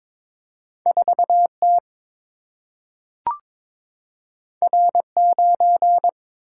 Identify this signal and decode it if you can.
Morse code, heard as a keyed tone, about 22 wpm, 700 Hz. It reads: 4 T R 9
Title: Single-Letter Single-Number 22wpm